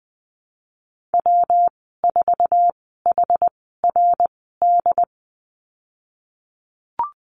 Morse code, heard as a keyed tone, about 20 wpm, spaced as for 14 wpm, 700 Hz. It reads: W4HRD
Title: 154 Lesson 25g - Letter p - Call Signs 20wpm